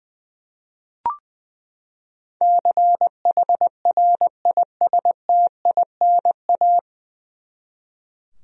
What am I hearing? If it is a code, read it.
Morse code, heard as a keyed tone, about 21 wpm, 700 Hz. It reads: CHRISTINA
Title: USA Common Names - 20wpm